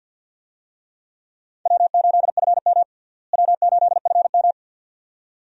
Morse code, heard as a keyed tone, about 52 wpm, 700 Hz. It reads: W8FM W8FM
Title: USA Call Signs - 4 Characters - Repeated 2x - 50wpm